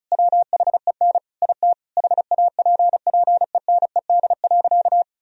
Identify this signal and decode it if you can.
Morse code, heard as a keyed tone, about 36 wpm, 700 Hz. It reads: WHEN IT HAPPENED.
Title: Sentences from Top 1000 Words - Review Part 4 - 35wpm